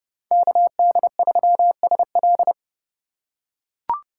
Morse code, heard as a keyed tone, about 30 wpm, 700 Hz. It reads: KD3SL
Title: USA Call Signs - 5 Characters - 4x Character Spacing - 30wpm